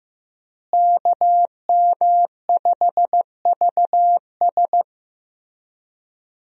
Morse code, heard as a keyed tone, about 15 wpm, 700 Hz. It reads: KM5VS
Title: USA Call Signs - 5 Characters - Repeated 3x - 15wpm